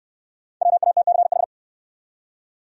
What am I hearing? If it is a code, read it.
Morse code, heard as a keyed tone, about 65 wpm, 700 Hz. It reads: WATCH